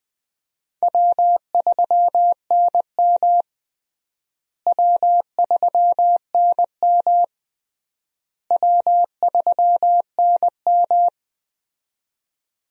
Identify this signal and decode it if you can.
Morse code, heard as a keyed tone, about 20 wpm, 700 Hz. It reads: W3NM W3NM W3NM
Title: USA Call Signs - 4 Characters - Repeated 3x - 20wpm